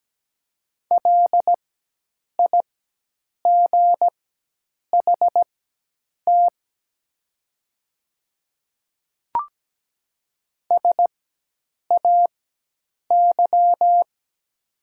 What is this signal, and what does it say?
Morse code, heard as a keyed tone, about 17 wpm, spaced as for 8 wpm, 700 Hz. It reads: LIGHT SAY
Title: 4x Character Spacing - Top 300 Words - 17wpm